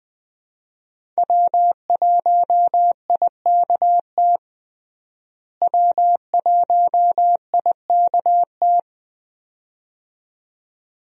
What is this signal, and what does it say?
Morse code, heard as a keyed tone, about 21 wpm, 700 Hz. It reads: W1IKT W1IKT
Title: USA Call Signs - 5 Characters - Repeated 2x - 20wpm